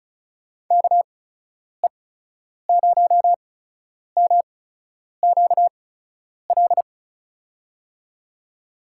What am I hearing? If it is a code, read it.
Morse code, heard as a keyed tone, about 36 wpm, spaced as for 10 wpm, 700 Hz. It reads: KE0MQL
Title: USA Call Signs - 6 Characters - 8x Character Spacing - 35wpm